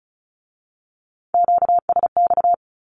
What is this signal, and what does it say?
Morse code, heard as a keyed tone, about 36 wpm, 700 Hz. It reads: QSX
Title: Q-Codes with Spoken Meaning - 35wpm